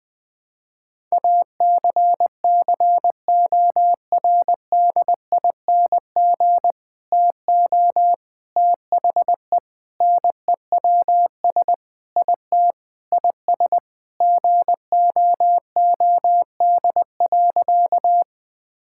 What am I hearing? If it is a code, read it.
Morse code, heard as a keyed tone, about 20 wpm, 700 Hz. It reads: ACCORDING TO THE NEWS IT IS GOOD.
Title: Sentences with 3-Word Phrases - Spoken Beginning - 20wpm